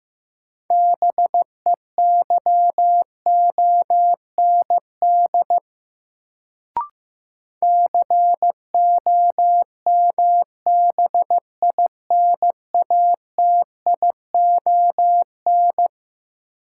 Morse code, heard as a keyed tone, about 15 wpm, 700 Hz. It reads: BEYOND COMBINATION
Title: Sentences from Top 1000 Words - Review Part 6 - New Word Drill - 15wpm